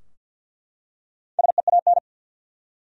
Morse code, heard as a keyed tone, about 51 wpm, 700 Hz. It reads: SEAN
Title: USA Common Names - Max 4 Letters - 50wpm